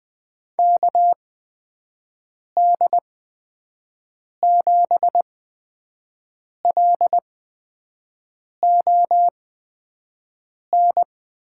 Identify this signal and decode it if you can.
Morse code, heard as a keyed tone, about 20 wpm, spaced as for 5 wpm, 700 Hz. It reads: KD7LON
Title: USA Call Signs - 6 Characters - 8x Character Spacing - 20wpm